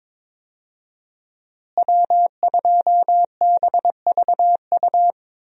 Morse code, heard as a keyed tone, about 22 wpm, 700 Hz. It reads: W2BVU
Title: USA Call Signs - 5 Characters - Repeated 2x - 22wpm